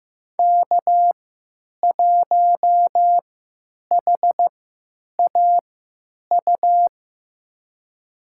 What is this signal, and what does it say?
Morse code, heard as a keyed tone, about 15 wpm, spaced as for 8 wpm, 700 Hz. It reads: K1HAU
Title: USA Call Signs - 5 Characters - 3x Character Spacing - 15wpm